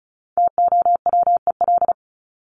Morse code, heard as a keyed tone, about 34 wpm, 700 Hz. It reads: TOWEL
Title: Top 1501-1750 Words 35wpm